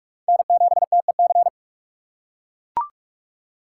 Morse code, heard as a keyed tone, about 41 wpm, 700 Hz. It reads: N7TEC
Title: USA Call Signs - 6x Character Spacing - 45wpm